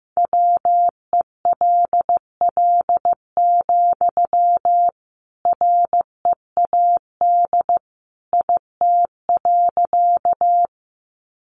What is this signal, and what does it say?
Morse code, heard as a keyed tone, about 15 wpm, 700 Hz. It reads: WELL, READ IT.
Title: Sentences from Top 200 Words 15wpm